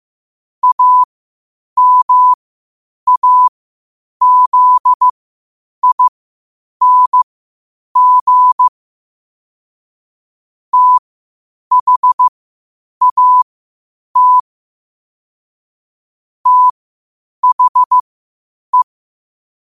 Morse code, heard as a keyed tone, about 15 wpm, spaced as for 8 wpm, 1000 Hz. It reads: AMAZING THAT THE
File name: qotdfiles / 8wpm